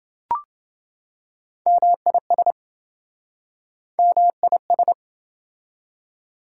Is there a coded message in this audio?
Yes: Morse code from 1.7 to 4.9 s, heard as a keyed tone, about 27 wpm, 700 Hz. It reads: MIS MIS